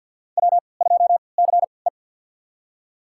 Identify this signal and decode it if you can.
Morse code, heard as a keyed tone, about 51 wpm, spaced as for 28 wpm, 700 Hz. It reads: W2CE
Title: USA Call Signs - 4 Characters - 3x Character Spacing - 50wpm